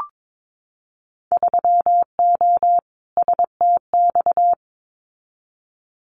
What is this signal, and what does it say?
Morse code, heard as a keyed tone, about 22 wpm, 700 Hz. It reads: 3O STX
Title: Winter Field Day Exchanges 22wpm